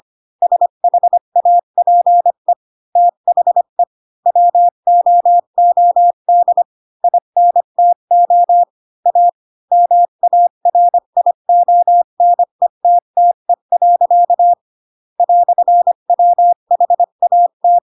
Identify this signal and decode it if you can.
Morse code, heard as a keyed tone, about 25 wpm, 700 Hz. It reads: SHAPE THE WOOD INTO A MARIONETTE. "WHAT
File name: Pinocchio / Pinocchio ch3 20 25 ik1hge